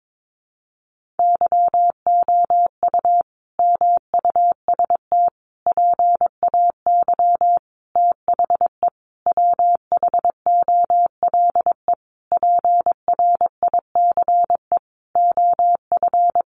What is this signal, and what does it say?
Morse code, heard as a keyed tone, about 22 wpm, 700 Hz. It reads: YOU MUST PAY THE WHOLE PRICE OF
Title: Sentences from Top 1400 Words 22wpm